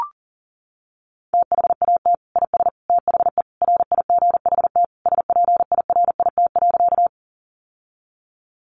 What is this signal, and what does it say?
Morse code, heard as a keyed tone, about 40 wpm, 700 Hz. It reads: THAT IS THE RIGHT SPIRIT.
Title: Sentences from Top 1400 Words 40wpm